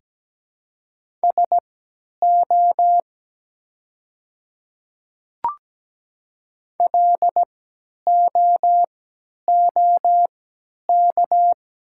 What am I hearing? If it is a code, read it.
Morse code, heard as a keyed tone, about 17 wpm, spaced as for 10 wpm, 700 Hz. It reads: SO LOOK
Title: 3x Character Spacing - Top 100 Words - 17wpm